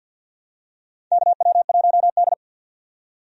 Morse code, heard as a keyed tone, about 48 wpm, 700 Hz. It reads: KW1D
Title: USA Call Signs - 4 Characters - 50wpm